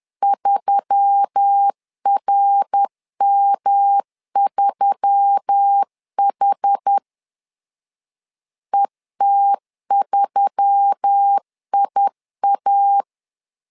Morse code, heard as a keyed tone, about 10 wpm, 800 Hz. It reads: 3RM3H ET3IA